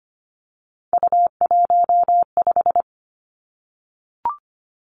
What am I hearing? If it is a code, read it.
Morse code, heard as a keyed tone, about 25 wpm, 700 Hz. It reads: U15
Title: Three Challenging Letter-Number Combinations 25wpm